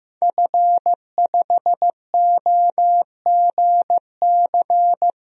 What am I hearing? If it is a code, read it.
Morse code, heard as a keyed tone, about 15 wpm, 700 Hz. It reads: F5OGC
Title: DXCC-100 Call Signs - 5 Characters - Repeated 2x - 15wpm